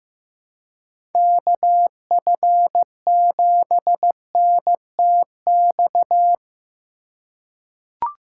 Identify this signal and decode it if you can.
Morse code, heard as a keyed tone, about 15 wpm, 700 Hz. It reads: KF7NTX
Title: USA Call Signs - 6 Characters - 2x Character Spacing - 15wpm